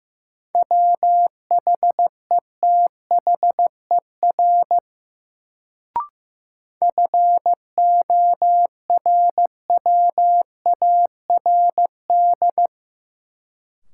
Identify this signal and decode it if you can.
Morse code, heard as a keyed tone, about 15 wpm, 700 Hz. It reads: WHETHER FORWARD
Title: Sentences from Top 1300 Words - New Word Drill Alt - 15wpm